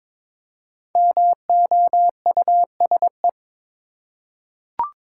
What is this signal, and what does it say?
Morse code, heard as a keyed tone, about 22 wpm, 690 Hz. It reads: MOUSE
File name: 5 Letter Words - First 3 Letters Given - 22wpm